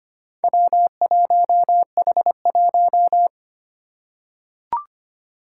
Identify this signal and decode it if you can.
Morse code, heard as a keyed tone, about 25 wpm, 700 Hz. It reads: W1H1
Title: Four Challenging Letter-Number Combinations 25wpm